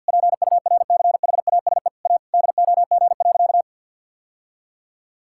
Morse code, heard as a keyed tone, about 47 wpm, 700 Hz. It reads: PURCHASE A DOG.
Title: Sentences from Top 1000 Words - Review Part 5 - 50wpm